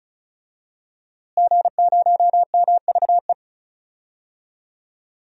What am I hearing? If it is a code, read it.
Morse code, heard as a keyed tone, about 36 wpm, 700 Hz. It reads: G0MVE
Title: DXCC-100 Call Signs - 5 Characters - Repeated 3x - 35wpm